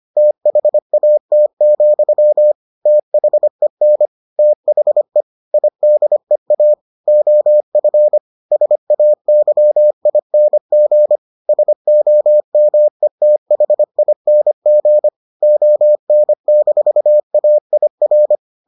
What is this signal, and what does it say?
Morse code, heard as a keyed tone, about 25 wpm, 600 Hz. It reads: THAT, THEN THE IDEA OF SAYING SOMETHING ON-AIR